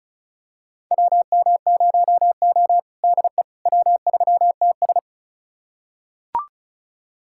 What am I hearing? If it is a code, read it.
Morse code, heard as a keyed tone, about 34 wpm, 700 Hz. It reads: WM0O DE W3TS
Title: USA Callsign DE Callsign - 4 Characters - 35wpm